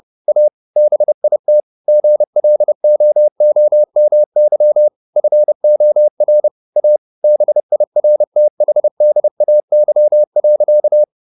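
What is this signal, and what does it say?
Morse code, heard as a keyed tone, about 30 wpm, 600 Hz. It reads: A BIT GLOOMY FOR A BIRTHDAY.
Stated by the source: Morse code at 30 wpm, the novel Never by Ken Follett in English, from CW med Gnister.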